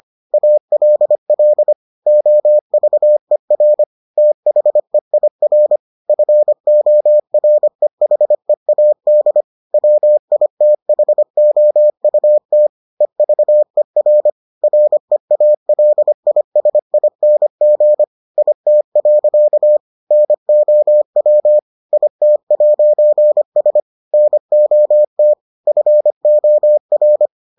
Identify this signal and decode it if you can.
Morse code, heard as a keyed tone, about 25 wpm, 600 Hz. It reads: ALL OVER THEIR FOREHEAD WITHOUT EVER REALISING IT. NOW IT'S NOT FOR